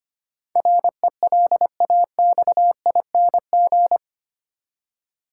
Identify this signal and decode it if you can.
Morse code, heard as a keyed tone, about 25 wpm, 700 Hz. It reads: RELAXING
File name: Eight Letter Words - 25wpm